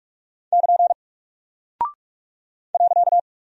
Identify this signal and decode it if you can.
Morse code, heard as a keyed tone, about 44 wpm, 700 Hz. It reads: <KN> .
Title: Single Character (Letter:Number:Punctuation:Prosign) 45wpm